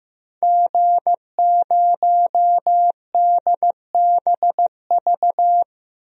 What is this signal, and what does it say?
Morse code, heard as a keyed tone, about 15 wpm, 700 Hz. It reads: G0DBV
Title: DXCC-100 Call Signs - 5 Characters - Repeated 2x - 15wpm